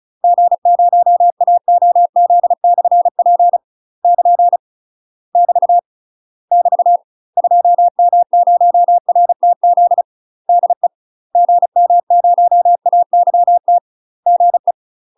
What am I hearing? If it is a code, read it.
Morse code, heard as a keyed tone, about 35 wpm, 700 Hz. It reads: G0AOZ/P <KN> <BT> <BT> 2M0RTZ DE GM0AYT GE
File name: UK MASTER QSOs TRAINING / 35WPM / UK MASTER QSOs 0001